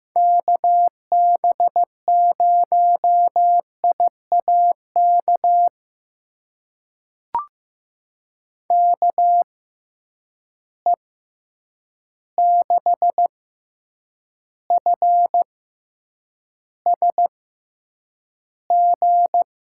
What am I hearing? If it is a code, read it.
Morse code, heard as a keyed tone, about 15 wpm, spaced as for 8 wpm, 700 Hz. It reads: KB0IAK KE6FSG
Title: USA Call Signs - 6 Characters - 6x Character Spacing - 15wpm